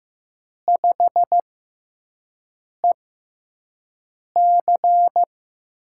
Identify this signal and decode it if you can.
Morse code, heard as a keyed tone, about 15 wpm, spaced as for 5 wpm, 700 Hz. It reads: 5EC
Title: USA Call Signs - 5 Characters - 6x Character Spacing - 15wpm